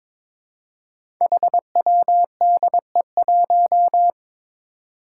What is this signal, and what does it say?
Morse code, heard as a keyed tone, about 22 wpm, 700 Hz. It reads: HWDE1
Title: Five Challenging Letter-Number Combinations 22wpm